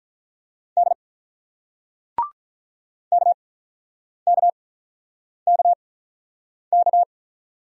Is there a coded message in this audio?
Yes: Morse code from 0.8 to 7.0 s, heard as a keyed tone, about 47 wpm, spaced as for 9 wpm, 700 Hz. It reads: D KKKK